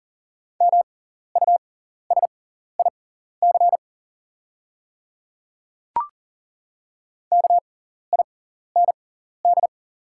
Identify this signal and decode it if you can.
Morse code, heard as a keyed tone, about 40 wpm, spaced as for 14 wpm, 700 Hz. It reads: MUSIC KIND
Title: 6x Character Spacing - Top 500 Words - 40wpm